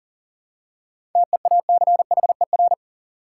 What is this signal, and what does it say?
Morse code, heard as a keyed tone, about 40 wpm, 700 Hz. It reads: TEACHER
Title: Seven Letter Words - 40wpm